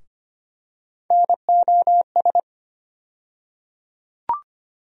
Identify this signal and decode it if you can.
Morse code, heard as a keyed tone, about 25 wpm, 700 Hz. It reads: NOS
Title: USA Call Sign Postfixes - 3 Letters 25wpm